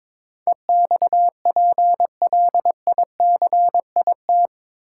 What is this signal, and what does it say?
Morse code, heard as a keyed tone, about 22 wpm, 700 Hz. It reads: EXPLICIT
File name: practice / 22wpm / Eight Letter Words - 22wpm